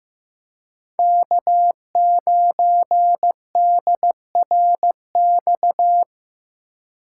Morse code, heard as a keyed tone, about 15 wpm, 700 Hz. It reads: K9DRX
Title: USA Call Signs - 5 Characters - 3x Character Spacing - 15wpm